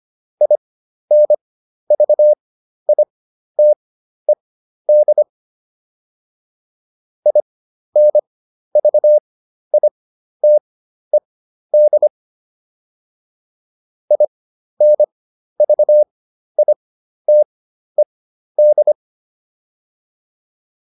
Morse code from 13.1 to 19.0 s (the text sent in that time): INVITED